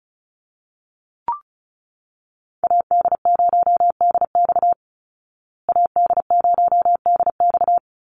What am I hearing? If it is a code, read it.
Morse code, heard as a keyed tone, about 34 wpm, 700 Hz. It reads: AD0DX AD0DX
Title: Call Signs Repeated 3x - 35wpm